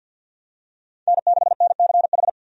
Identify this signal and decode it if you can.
Morse code, heard as a keyed tone, about 50 wpm, 700 Hz. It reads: N6NCH
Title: USA Call Signs - 5 Characters - 50wpm